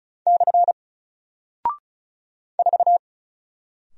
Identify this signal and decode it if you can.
Morse code, heard as a keyed tone, about 36 wpm, 700 Hz. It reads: / 4
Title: Single Character (Letter:Number:Punctuation:Prosign) 35wpm